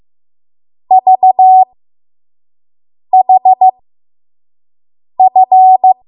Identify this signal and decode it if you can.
Morse code, heard as a keyed tone, about 15 wpm, spaced as for 5 wpm, 750 Hz. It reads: VHF
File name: Morse / 05WPM